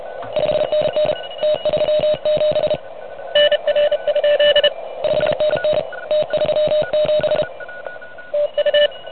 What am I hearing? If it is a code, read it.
Morse code, heard as a keyed tone, about 30 wpm, 600 Hz. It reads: 5NN T37 NR? 5NN T37 TU
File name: truSDX / MWC29.8